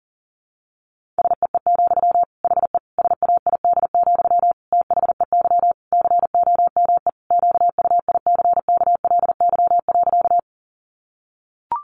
Morse code, heard as a keyed tone, about 40 wpm, 700 Hz. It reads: SEE, HE SAID, THEY COME QUICKLY.
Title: Sentences from Top 700 Words 40wpm